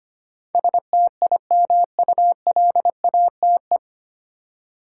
Morse code, heard as a keyed tone, about 25 wpm, 700 Hz. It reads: STIMULATE
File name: practice / 25wpm / 9 Letter Words - First 3 Letters Given - 25wpm